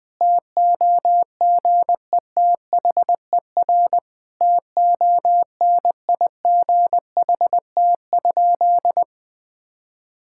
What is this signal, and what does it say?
Morse code, heard as a keyed tone, about 20 wpm, 700 Hz. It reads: TOGETHER TONIGHT?
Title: Sentences from Top 1600 Words 20wpm